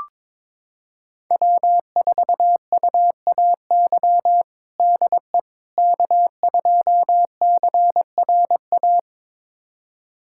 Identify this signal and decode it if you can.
Morse code, heard as a keyed tone, about 21 wpm, 700 Hz. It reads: W4UAY DE K2CRA
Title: USA Callsign DE Callsign - 5 Characters - 22wpm